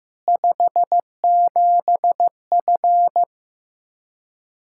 Morse code, heard as a keyed tone, about 15 wpm, 700 Hz. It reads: 57F
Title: Practice QSO - Exchange 2 - Info - 15wpm